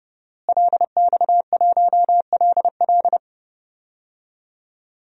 Morse code, heard as a keyed tone, about 31 wpm, 700 Hz. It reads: LX1LL
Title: DXCC-100 Call Signs - 5 Characters - Repeated 2x - 30wpm